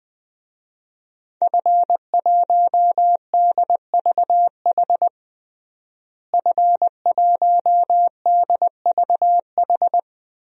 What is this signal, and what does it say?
Morse code, heard as a keyed tone, about 20 wpm, 700 Hz. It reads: F1DVH F1DVH
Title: DXCC-100 Call Signs - 5 Characters - Repeated 2x - 20wpm